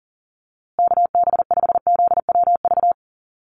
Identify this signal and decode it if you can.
Morse code, heard as a keyed tone, about 40 wpm, 700 Hz. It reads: KB5ZWV
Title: USA Call Signs Repeated 3x 40wpm